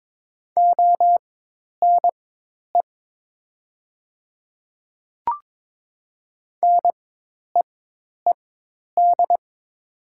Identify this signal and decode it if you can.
Morse code, heard as a keyed tone, about 23 wpm, spaced as for 10 wpm, 700 Hz. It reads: ONE NEED
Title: 4x Character Spacing - Top 200 Words - 22wpm